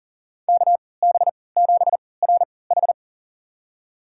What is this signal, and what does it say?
Morse code, heard as a keyed tone, about 40 wpm, spaced as for 23 wpm, 700 Hz. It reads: KB7RH
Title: USA Call Signs - 3x Character Spacing - 40wpm